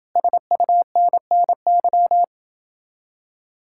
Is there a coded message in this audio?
Yes: Morse code from 0.2 to 2.2 s, heard as a keyed tone, about 28 wpm, 700 Hz. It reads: SUNNY